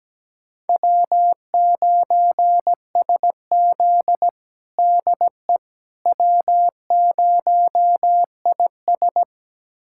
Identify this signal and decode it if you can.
Morse code, heard as a keyed tone, about 17 wpm, 700 Hz. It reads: W9SZ DE W0IS
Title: USA Callsign DE Callsign - 4 Characters - 17wpm